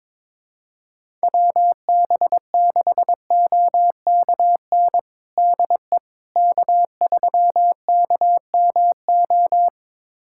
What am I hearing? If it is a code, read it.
Morse code, heard as a keyed tone, about 22 wpm, 700 Hz. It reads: WB6OKN DE K3KMO
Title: USA Callsign DE Callsign 22wpm